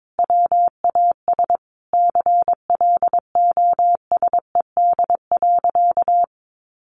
Morse code, heard as a keyed tone, about 22 wpm, 700 Hz. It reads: WAS CLOSED.
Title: Sentences from Top 1400 Words 22wpm